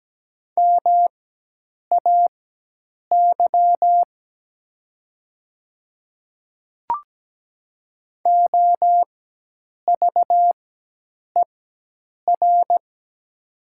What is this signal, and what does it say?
Morse code, heard as a keyed tone, about 17 wpm, spaced as for 8 wpm, 700 Hz. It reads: MAY OVER
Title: 4x Character Spacing - Top 100 Words - 17wpm